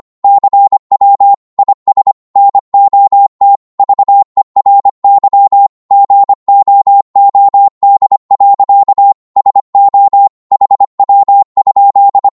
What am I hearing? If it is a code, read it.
Morse code, heard as a keyed tone, about 25 wpm, 800 Hz. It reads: CW IS NOT VERY GOOD. SO HW?
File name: Morse / qso-25wpm